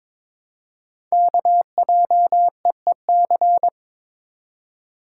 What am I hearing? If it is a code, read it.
Morse code, heard as a keyed tone, about 22 wpm, 700 Hz. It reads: KJEEC
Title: Code Groups - 5 Letters - 22wpm